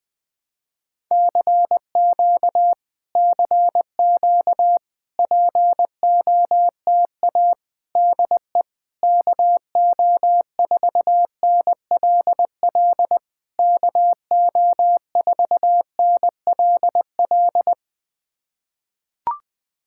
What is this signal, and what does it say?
Morse code, heard as a keyed tone, about 20 wpm, 700 Hz. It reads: CQ CQ POTA DE KO4NLL KO4NLL
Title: POTA QSO Practice - 20wpm